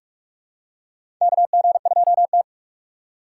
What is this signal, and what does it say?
Morse code, heard as a keyed tone, about 46 wpm, 700 Hz. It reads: KG2T
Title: USA Call Signs - 6x Character Spacing - 45wpm